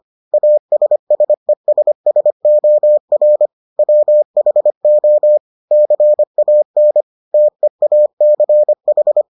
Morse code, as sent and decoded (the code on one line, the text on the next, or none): .- ... ... ... ... --- .-. / .-- .... --- / -.-. .- -. / - . .- -.-. ....
ASSESSOR WHO CAN TEACH